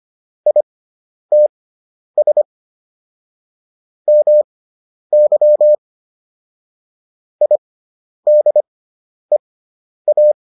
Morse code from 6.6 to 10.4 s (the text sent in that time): IDEA